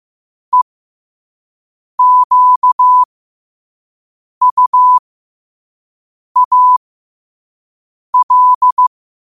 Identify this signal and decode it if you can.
Morse code, heard as a keyed tone, about 15 wpm, spaced as for 5 wpm, 1000 Hz. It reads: EQUAL
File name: 5wpm